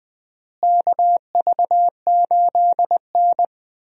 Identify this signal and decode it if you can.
Morse code, heard as a keyed tone, about 21 wpm, 700 Hz. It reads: KV8N
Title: USA Call Signs - 4 Characters - Repeated 3x - 20wpm